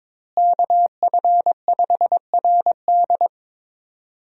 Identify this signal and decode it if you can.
Morse code, heard as a keyed tone, about 21 wpm, 700 Hz. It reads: KF5RD
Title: USA Call Signs - 5 Characters - Repeated 2x - 22wpm